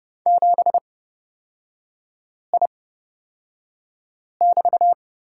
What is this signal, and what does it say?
Morse code, heard as a keyed tone, about 30 wpm, spaced as for 5 wpm, 700 Hz. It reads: 7I<BT>
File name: Lesson 37e - Letter x - Three Characters 30wpm